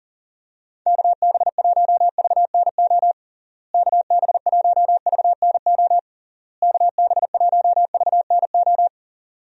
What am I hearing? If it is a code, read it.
Morse code, heard as a keyed tone, about 40 wpm, 700 Hz. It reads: KB1VNO KB1VNO KB1VNO
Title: USA Call Signs - 6 Characters - Repeated 3x - 40wpm